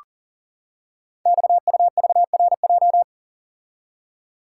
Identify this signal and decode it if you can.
Morse code, heard as a keyed tone, about 40 wpm, 700 Hz. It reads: XUVRJ
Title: Code Groups - 5 Letters - 40wpm